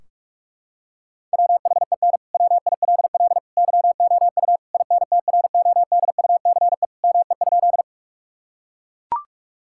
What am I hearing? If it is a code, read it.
Morse code, heard as a keyed tone, about 41 wpm, 700 Hz. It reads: WHEN WILL YOU INTRODUCE ME?
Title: Sentences from Top 1600 Words 45wpm